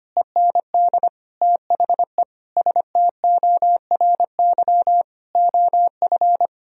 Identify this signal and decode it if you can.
Morse code, heard as a keyed tone, about 25 wpm, 700 Hz. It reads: END THE STORY OF
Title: Sentences from Top 1000 Words - Review Part 1 - 25wpm